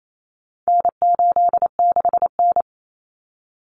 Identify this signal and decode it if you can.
Morse code, heard as a keyed tone, about 28 wpm, 700 Hz. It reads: N86N
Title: Common Cut Numbers - 4-Digit 28wpm